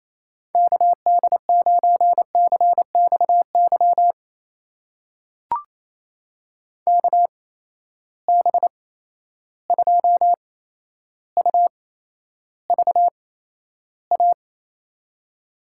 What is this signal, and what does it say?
Morse code, heard as a keyed tone, about 28 wpm, spaced as for 12 wpm, 700 Hz. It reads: KD9CXY KB2UVA
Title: USA Call Signs - 6 Characters - 8x Character Spacing - 28wpm